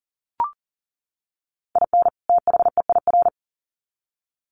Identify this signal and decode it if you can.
Morse code, heard as a keyed tone, about 40 wpm, 700 Hz. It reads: IN THEIR
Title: Sets of 2 Words 40wpm